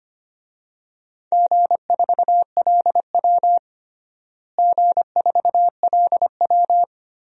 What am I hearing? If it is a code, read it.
Morse code, heard as a keyed tone, about 25 wpm, 700 Hz. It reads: G4LW G4LW
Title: DXCC-100 Call Signs - 4 Characters - Repeated 2x - 25wpm